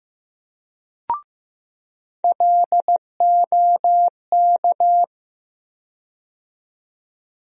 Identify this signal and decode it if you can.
Morse code, heard as a keyed tone, about 15 wpm, 700 Hz. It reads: LOK
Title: USA Call Sign Postfixes - 3 Letters 15wpm